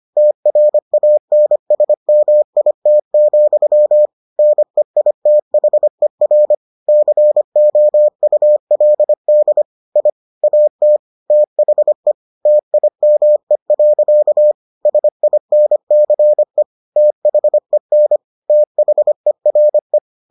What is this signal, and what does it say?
Morse code, heard as a keyed tone, about 25 wpm, 600 Hz. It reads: TRANSMIT, NEITHER COULD I AT THE TIME. SINCE THEN THERE